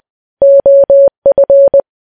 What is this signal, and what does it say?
Morse code, heard as a keyed tone, about 20 wpm, 560 Hz. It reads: OF
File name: morse